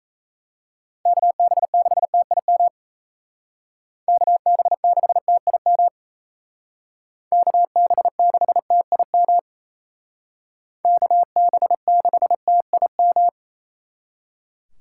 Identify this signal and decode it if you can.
Morse code, heard as a keyed tone, about 33 wpm, 700 Hz. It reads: KB6TIM KB6TIM KB6TIM KB6TIM
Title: Speed Racing - USA Call Signs 28wpm